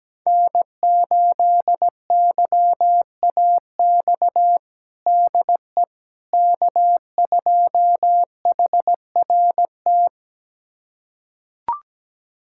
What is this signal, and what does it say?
Morse code, heard as a keyed tone, about 17 wpm, 700 Hz. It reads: N8YAX DE K2HRT
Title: USA Callsign DE Callsign - 5 Characters - 17wpm